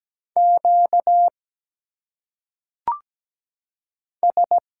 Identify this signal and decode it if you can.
Morse code, heard as a keyed tone, about 17 wpm, 700 Hz. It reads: Q S
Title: Single Letters - Phonetic Pronunciation 17wpm